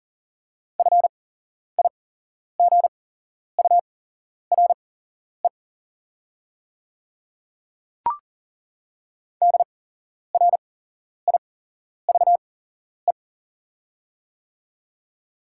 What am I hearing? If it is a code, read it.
Morse code, heard as a keyed tone, about 40 wpm, spaced as for 11 wpm, 690 Hz. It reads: FIGURE DRIVE
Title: 8x Character Spacing - Top 500 Words - 40wpm